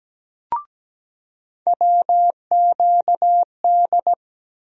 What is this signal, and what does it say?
Morse code, heard as a keyed tone, about 17 wpm, 700 Hz. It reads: WQD
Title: USA Call Sign Postfixes - 3 Letters 17wpm